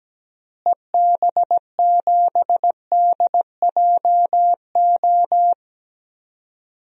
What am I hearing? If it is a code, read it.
Morse code, heard as a keyed tone, about 17 wpm, 700 Hz. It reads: EB7DJO
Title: DXCC-100 Call Signs - 6 Characters - Repeated 2x - 17wpm